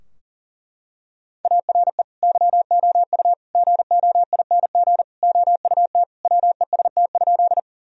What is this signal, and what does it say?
Morse code, heard as a keyed tone, about 40 wpm, 700 Hz. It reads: ARE YOU GOING OUT WEST?
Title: Sentences from Top 1000 Words - Review Part 2 - 40wpm